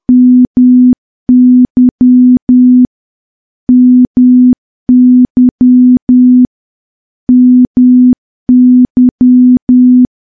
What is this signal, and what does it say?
Morse code, heard as a keyed tone, about 10 wpm, 250 Hz. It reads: MY MY MY
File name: morse / morse-training-10WPM